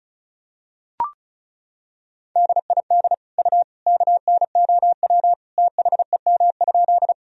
Morse code, heard as a keyed tone, about 36 wpm, 700 Hz. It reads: DID U KNOW THEM?
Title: Sentences from Top 100 Words and Abbreviations 35wpm